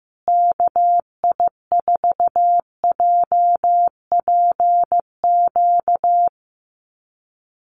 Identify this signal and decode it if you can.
Morse code, heard as a keyed tone, about 15 wpm, 700 Hz. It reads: KI4JPQ
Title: USA Call Signs Repeated 3x 15wpm